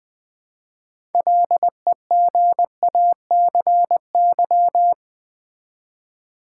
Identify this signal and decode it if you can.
Morse code, heard as a keyed tone, about 20 wpm, 700 Hz. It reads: LEGACY
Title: Build A Word - 2 Letters At A Time - 6-Letter Words - 20wpm